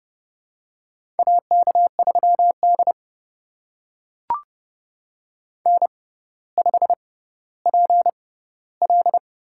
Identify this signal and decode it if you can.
Morse code, heard as a keyed tone, about 29 wpm, spaced as for 15 wpm, 700 Hz. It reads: AK3D N5PL
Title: USA Call Signs - 4 Characters - 6x Character Spacing - 30wpm